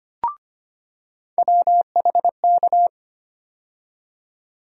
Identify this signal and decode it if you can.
Morse code, heard as a keyed tone, about 25 wpm, 700 Hz. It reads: WHK
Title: USA Call Sign Postfixes - 3 Letters 25wpm